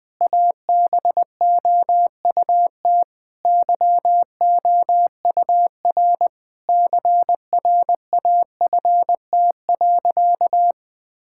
Morse code, heard as a keyed tone, about 20 wpm, 700 Hz. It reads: ABOUT YOUR CRAFT.
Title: Sentences from Top 1000 Words - Review Part 6 - 20wpm